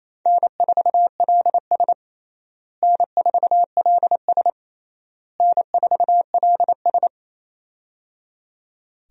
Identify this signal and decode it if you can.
Morse code, heard as a keyed tone, about 28 wpm, 700 Hz. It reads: N4LS N4LS N4LS
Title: USA Call Signs - 4 Characters - Repeated 3x - 28wpm